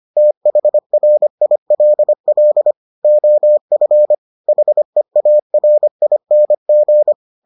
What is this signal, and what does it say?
Morse code, heard as a keyed tone, about 25 wpm, 600 Hz. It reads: THRILL OF HEARING